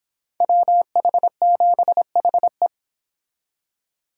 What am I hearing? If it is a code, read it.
Morse code, heard as a keyed tone, about 30 wpm, 700 Hz. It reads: WH7HE